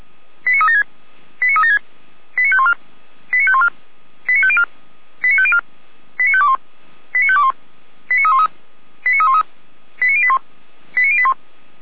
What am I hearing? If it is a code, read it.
Five-tone selective call (ZVEI), heard as quick runs of five tones, tones of about 70 ms. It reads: 89376 89376 89514 89514 89504 89504 89521 89521 89313 89313 89081 89081
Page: ZVEI